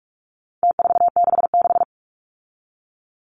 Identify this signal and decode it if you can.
Morse code, heard as a keyed tone, about 44 wpm, 700 Hz. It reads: T466
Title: Numbers - Contest Style - 4-Digit 45wpm